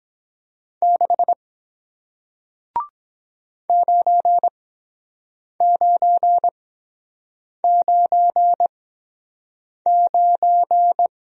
Morse code, heard as a keyed tone, about 24 wpm, spaced as for 7 wpm, 700 Hz. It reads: 6 9999